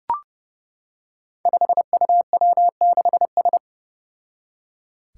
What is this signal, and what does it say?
Morse code, heard as a keyed tone, about 30 wpm, 700 Hz. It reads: 5UW6S
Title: Five Challenging Letter-Number Combinations 30wpm